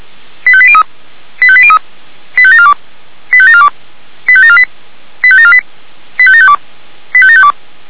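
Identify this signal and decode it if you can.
Five-tone selective call (ZVEI), heard as quick runs of five tones, tones of about 70 ms. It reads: 85803 85803 85742 85742 85748 85748 85752 85752
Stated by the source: ZVEI five-tone selective call received on an SDR.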